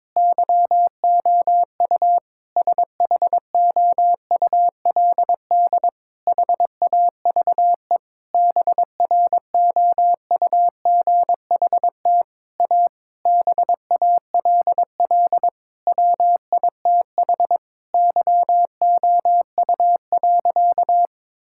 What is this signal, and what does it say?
Morse code, heard as a keyed tone, about 22 wpm, 700 Hz. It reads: YOU SHOULD HAVE BROUGHT A BALL WITH YOU.
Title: Sentences from Top 1000 Words - Review Part 6 - 22wpm